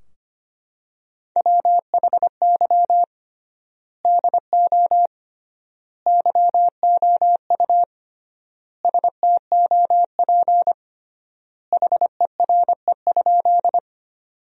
Morse code, heard as a keyed tone, about 25 wpm, 700 Hz. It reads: WHY DO YOU STOP HERE?
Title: 3x Word-Spacing - Sentences from Top 300 Words - 25wpm